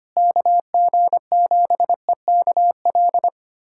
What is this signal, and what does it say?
Morse code, heard as a keyed tone, about 25 wpm, 700 Hz. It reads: KG7EKL
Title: USA Callsign DE Callsign 25wpm